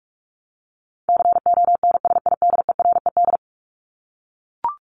Morse code, heard as a keyed tone, about 44 wpm, 700 Hz. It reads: CONSIDERED
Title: ED Suffix - Top 100 Words with Base Words 45wpm